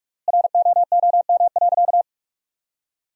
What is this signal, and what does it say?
Morse code, heard as a keyed tone, about 47 wpm, 700 Hz. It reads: ROOM.
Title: 3x Word-Spacing - Sentences from Top 300 Words - 45wpm